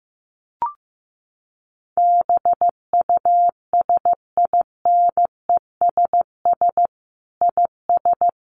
Morse code, heard as a keyed tone, about 15 wpm, 700 Hz. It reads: BUSINESS IS
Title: Sentences from Top 700 Words 15wpm